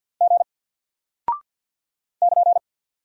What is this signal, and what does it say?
Morse code, heard as a keyed tone, about 48 wpm, 700 Hz. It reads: G <KN>
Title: Single Character (Letter:Number:Punctuation:Prosign) Alt 50wpm